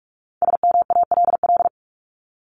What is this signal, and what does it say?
Morse code, heard as a keyed tone, about 44 wpm, 700 Hz. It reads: SMALL
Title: Top 200 Words 45wpm